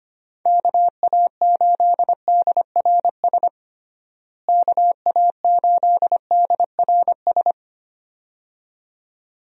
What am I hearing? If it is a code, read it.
Morse code, heard as a keyed tone, about 25 wpm, 700 Hz. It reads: KA8DRS KA8DRS
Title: USA Call Signs - 6 Characters - Repeated 2x - 25wpm